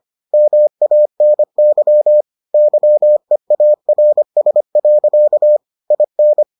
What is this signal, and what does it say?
Morse code, heard as a keyed tone, about 25 wpm, 600 Hz. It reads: MANY YEARS. IN